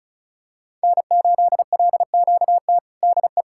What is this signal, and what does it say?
Morse code, heard as a keyed tone, about 34 wpm, 700 Hz. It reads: N8LQT DE
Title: K1USN SST QSO Practice 35wpm